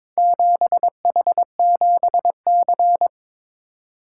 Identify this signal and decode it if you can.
Morse code, heard as a keyed tone, about 22 wpm, 700 Hz. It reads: 7H7C
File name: practice / 22wpm / Four Letter-Number Combinations - 22wpm